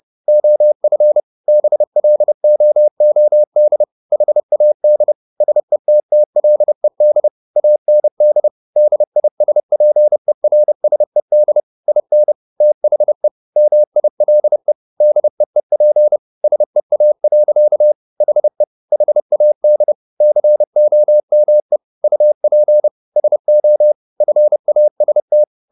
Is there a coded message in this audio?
Yes: Morse code from 0.3 to 25.4 s, heard as a keyed tone, about 30 wpm, 600 Hz. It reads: OF BLOOD HAD SETTLED AND DISPERSED IN THE MILE DEEP SEA. HE HAD COME UP SO FAST